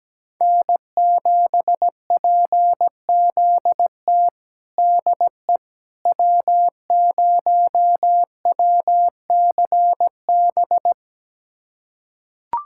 Morse code, heard as a keyed tone, about 17 wpm, 700 Hz. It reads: N7PZT DE W0WCB
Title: USA Callsign DE Callsign 17wpm